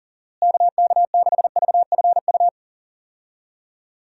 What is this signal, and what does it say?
Morse code, heard as a keyed tone, about 37 wpm, 700 Hz. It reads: KK6VFU
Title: USA Call Signs - 6 Characters - Repeated 3x - 40wpm